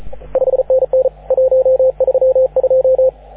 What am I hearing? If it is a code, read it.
Morse code, heard as a keyed tone, about 32 wpm, 530 Hz, with radio noise. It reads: 5NN 132
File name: AUDIO2019 / HG6N CW SSB MIX